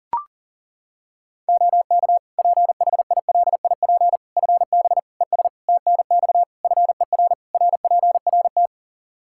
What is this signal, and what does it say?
Morse code, heard as a keyed tone, about 40 wpm, 700 Hz. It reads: OK PHILIP FB ES TNX FER RPRT
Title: K1USN SST QSO Practice 40wpm